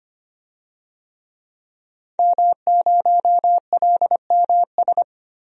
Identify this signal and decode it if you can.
Morse code, heard as a keyed tone, about 25 wpm, 700 Hz. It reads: M0LMS
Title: DXCC-100 Call Signs - 5 Characters - Repeated 2x - 25wpm